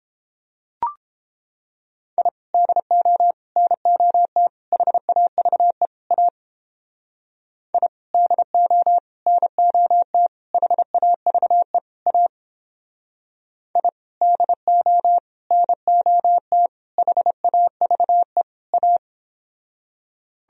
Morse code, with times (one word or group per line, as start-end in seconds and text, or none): 2.2-2.3 I
2.5-3.3 DO
3.6-4.5 NOT
4.7-5.9 HAVE
6.1-6.3 A
7.7-7.9 I
8.1-9.0 DO
9.3-10.3 NOT
10.5-11.8 HAVE
12.1-12.3 A
13.8-13.9 I
14.2-15.2 DO
15.5-16.7 NOT
17.0-18.4 HAVE
18.7-19.0 A